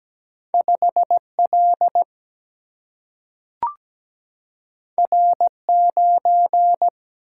Morse code, heard as a keyed tone, about 17 wpm, 700 Hz. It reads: 5 L R 9
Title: Single-Letter Single-Number 17wpm